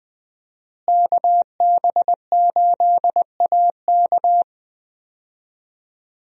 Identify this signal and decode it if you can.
Morse code, heard as a keyed tone, about 20 wpm, 700 Hz. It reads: KB8AK
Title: USA Call Signs - 5 Characters - Repeated 3x - 20wpm